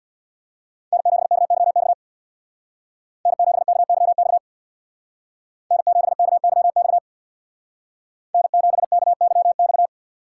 Morse code, heard as a keyed tone, about 64 wpm, 700 Hz. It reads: N7KYX N7KYX N7KYX N7KYX